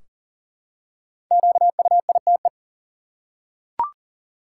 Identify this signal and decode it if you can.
Morse code, heard as a keyed tone, about 40 wpm, 700 Hz. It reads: QUITE
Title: Five Letter Words - 40wpm